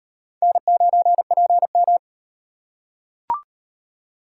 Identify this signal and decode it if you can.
Morse code, heard as a keyed tone, about 38 wpm, 700 Hz. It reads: N9PM